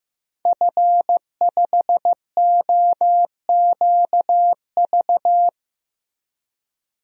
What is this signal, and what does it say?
Morse code, heard as a keyed tone, about 15 wpm, 700 Hz. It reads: F5OQV
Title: DXCC-100 Call Signs - 5 Characters - 15wpm